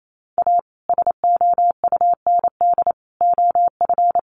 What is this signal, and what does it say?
Morse code, heard as a keyed tone, about 28 wpm, 700 Hz. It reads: A SOUND OF
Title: A Sound Of Thunder 28wpm